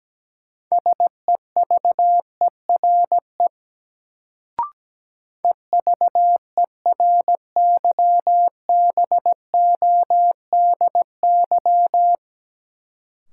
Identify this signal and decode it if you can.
Morse code, heard as a keyed tone, about 17 wpm, 700 Hz. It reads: SEVERE EVERYBODY
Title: Sentences from Top 1400 Words - New Word Drill - 17wpm